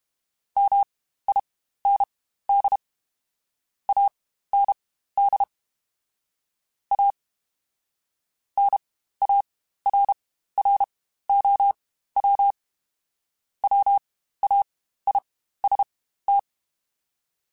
Morse code, heard as a keyed tone, about 32 wpm, spaced as for 15 wpm, 800 Hz. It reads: MIND AND A NARROW WAIST
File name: Spaced / Chuckles Spaced-mp3s / MiddleAge sp @30